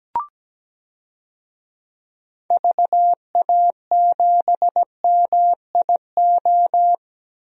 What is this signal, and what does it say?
Morse code, heard as a keyed tone, about 17 wpm, 700 Hz. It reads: VA7MIO
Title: DXCC-100 Call Signs - 6 Characters - Repeated 3x - 17wpm